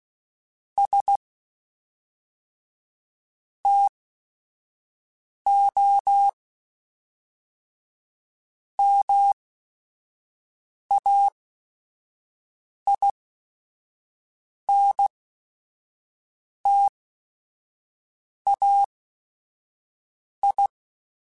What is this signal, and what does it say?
Morse code, heard as a keyed tone, about 16 wpm, spaced as for 5 wpm, 800 Hz. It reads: S TO MAINTAI
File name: Technical Subjects / Technical Subjects-mp3s / XL @5